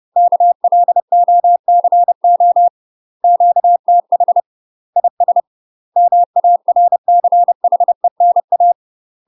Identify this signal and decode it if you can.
Morse code, heard as a keyed tone, about 30 wpm, 700 Hz. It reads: KLOCO QTH IS MARCHENA
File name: UK MASTER QSOs TRAINING / 30WPM / UK MASTER QSOs 0042